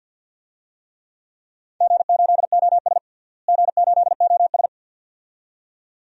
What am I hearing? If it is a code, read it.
Morse code, heard as a keyed tone, about 47 wpm, 700 Hz. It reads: G8OS G8OS
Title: DXCC-100 Call Signs - 4 Characters - Repeated 2x - 50wpm